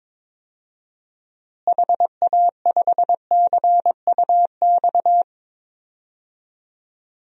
Morse code, heard as a keyed tone, about 22 wpm, 700 Hz. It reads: HA5CUX
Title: DXCC-100 Call Signs - 6 Characters - Repeated 2x - 22wpm